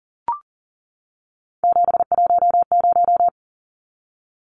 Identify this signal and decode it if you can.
Morse code, heard as a keyed tone, about 40 wpm, 700 Hz. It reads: 710
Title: Numbers - 3-Digit 40wpm